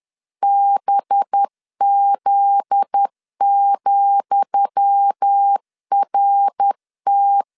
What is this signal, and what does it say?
Morse code, heard as a keyed tone, about 10 wpm, 800 Hz. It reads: BZ,RT
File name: LEZIONI / 10 SIGNS / 10D COMA 10wpm 3013